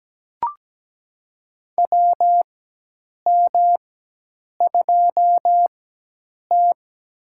Morse code, heard as a keyed tone, about 17 wpm, spaced as for 8 wpm, 700 Hz. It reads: WM2T
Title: USA Call Signs - 4 Characters - 4x Character Spacing - 17wpm